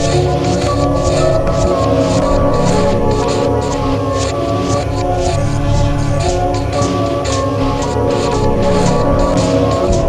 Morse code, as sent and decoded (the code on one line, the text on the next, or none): none